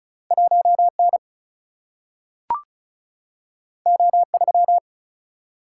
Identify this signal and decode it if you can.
Morse code, heard as a keyed tone, about 36 wpm, 700 Hz. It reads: 1N O3
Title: Single-Letter Single-Number 35wpm